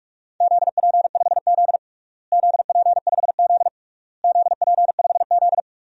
Morse code, heard as a keyed tone, about 46 wpm, 690 Hz. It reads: ZP5Z ZP5Z ZP5Z
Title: DXCC-100 Call Signs - 4 Characters - Repeated 3x - 45wpm